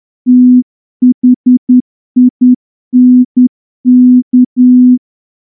Morse code, heard as a keyed tone, about 10 wpm, 250 Hz. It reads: THINK
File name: morse-training-10WPM